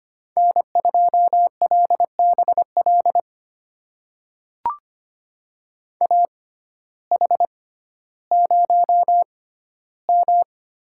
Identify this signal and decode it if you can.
Morse code, heard as a keyed tone, about 25 wpm, spaced as for 14 wpm, 700 Hz. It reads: N2LBL AH0M
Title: USA Call Signs - 6x Character Spacing - 25wpm